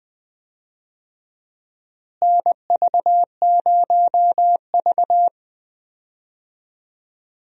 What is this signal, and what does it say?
Morse code, heard as a keyed tone, about 20 wpm, 700 Hz. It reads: NV0V
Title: USA Call Signs - 4 Characters - Repeated 2x - 20wpm